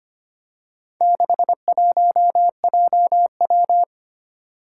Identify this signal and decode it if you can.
Morse code, heard as a keyed tone, about 25 wpm, 700 Hz. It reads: 61JW
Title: Four Challenging Letter-Number Combinations 25wpm